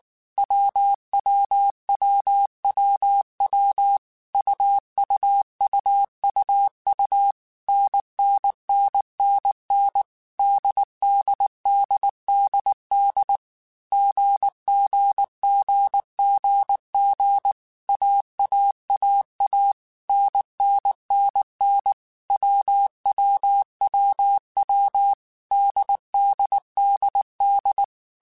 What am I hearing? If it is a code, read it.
Morse code, heard as a keyed tone, about 19 wpm, 800 Hz. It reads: WWWWW UUUUU NNNNN DDDDD GGGGG AAAA NNNN WWWW DDDD
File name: Groups@18 / Groups@18 mp3 / AWUNDG@18